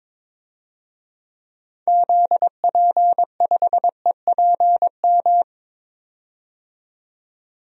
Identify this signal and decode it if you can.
Morse code, heard as a keyed tone, about 22 wpm, 700 Hz. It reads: ZP5EPM
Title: DXCC-100 Call Signs - 6 Characters - 22wpm